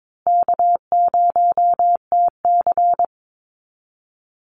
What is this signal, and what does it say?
Morse code, heard as a keyed tone, about 22 wpm, 700 Hz. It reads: K0TC
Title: Easy Call Signs Repeated 2x - 22wpm